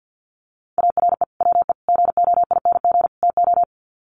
Morse code, heard as a keyed tone, about 50 wpm, 700 Hz. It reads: ARE WE GOING TO
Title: Sets of 4 Words 50wpm